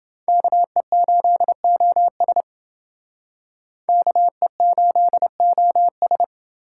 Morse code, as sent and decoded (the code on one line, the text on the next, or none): -.- . ---.. --- ... / -.- . ---.. --- ...
KE8OS KE8OS